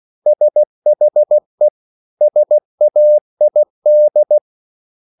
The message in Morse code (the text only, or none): SHE SAID